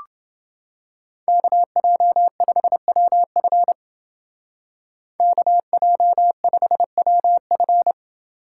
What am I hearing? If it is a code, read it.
Morse code, heard as a keyed tone, about 29 wpm, 700 Hz. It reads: KJ5WF KJ5WF